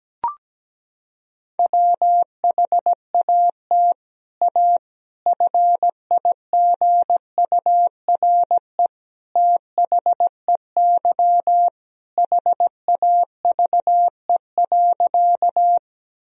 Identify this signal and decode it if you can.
Morse code, heard as a keyed tone, about 17 wpm, 700 Hz. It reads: WHAT A FIGURE THEY HAVE.
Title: Sentences from Top 1000 Words - Review Part 2 - 17wpm